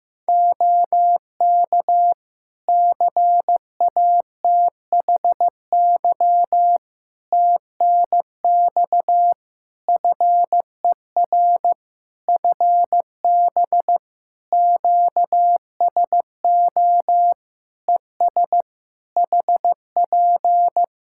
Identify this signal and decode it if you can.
Morse code, heard as a keyed tone, about 16 wpm, 700 Hz. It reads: OK CATHY TNX FER FB QSO ES HP
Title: K1USN SST QSO Practice 15wpm